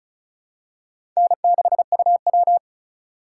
Morse code, heard as a keyed tone, about 34 wpm, 700 Hz. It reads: N6UW
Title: USA Call Signs - 3x Character Spacing - 35wpm